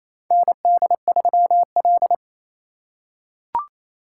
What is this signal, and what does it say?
Morse code, heard as a keyed tone, about 28 wpm, 700 Hz. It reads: ND3L
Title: K1USN SST QSO Practice - 28wpm